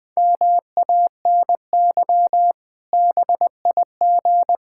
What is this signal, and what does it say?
Morse code, heard as a keyed tone, about 20 wpm, 700 Hz. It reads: MANY BIG
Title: Pangram Challenge v2 - 20wpm